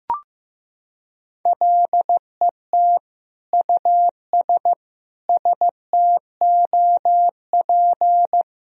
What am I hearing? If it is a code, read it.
Morse code, heard as a keyed tone, about 14 wpm, 700 Hz. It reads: LET US STOP
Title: Sentences from Top 400 Words and Abbreviations 15wpm